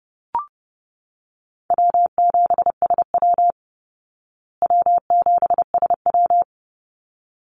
Morse code, heard as a keyed tone, about 30 wpm, 700 Hz. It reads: W7SW W7SW
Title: Easy Call Signs Repeated 2x - 30wpm